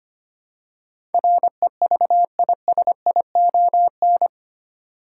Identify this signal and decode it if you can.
Morse code, heard as a keyed tone, about 25 wpm, 700 Hz. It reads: REVISION
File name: 8 Letter Words - First 3 Letters Given - 25wpm